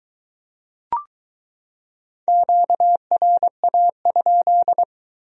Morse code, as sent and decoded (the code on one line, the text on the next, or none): --.- .-. .- ..--..
QRA?